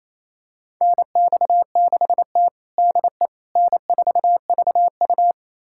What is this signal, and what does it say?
Morse code, heard as a keyed tone, about 28 wpm, 700 Hz. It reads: NX6T DE N4VU
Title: USA Callsign DE Callsign - 4 Characters - 28wpm